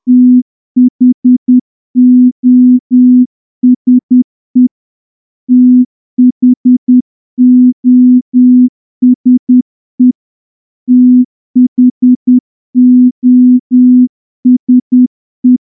Morse code, heard as a keyed tone, about 10 wpm, 250 Hz. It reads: THOSE THOSE THOSE
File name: morse-training-10WPM